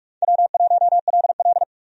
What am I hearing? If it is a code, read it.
Morse code, heard as a keyed tone, about 46 wpm, 700 Hz. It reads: W1LL
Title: USA Call Signs - 4 Characters - 45wpm